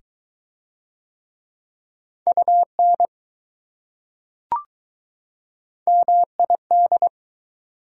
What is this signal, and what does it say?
Morse code, heard as a keyed tone, about 21 wpm, 700 Hz. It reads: UN MID